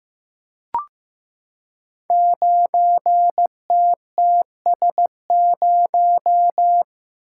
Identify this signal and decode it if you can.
Morse code, heard as a keyed tone, about 15 wpm, 700 Hz. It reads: 9TTS0
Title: Code Groups - 5 Letter:Numbers - 15wpm